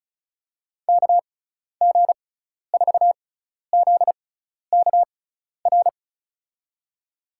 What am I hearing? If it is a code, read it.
Morse code, heard as a keyed tone, about 36 wpm, spaced as for 12 wpm, 700 Hz. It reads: KG4ZKR
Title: USA Call Signs - 6x Character Spacing - 35wpm